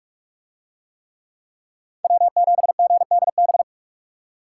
Morse code, heard as a keyed tone, about 42 wpm, 700 Hz. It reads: W7GDB